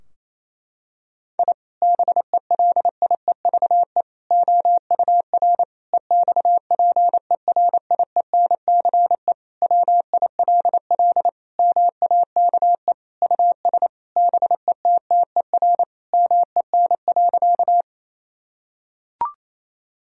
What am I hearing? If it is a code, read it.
Morse code, heard as a keyed tone, about 27 wpm, 700 Hz. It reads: I BELIEVE OUR EXPERIENCE WILL MAKE US BETTER MEN.
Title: Sentences from Top 1000 Words - Review Part 3 - 28wpm